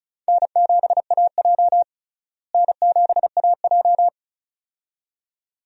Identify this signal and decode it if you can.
Morse code, heard as a keyed tone, about 36 wpm, 700 Hz. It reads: N7AJ N7AJ
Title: USA Call Signs - 4 Characters - Repeated 2x - 35wpm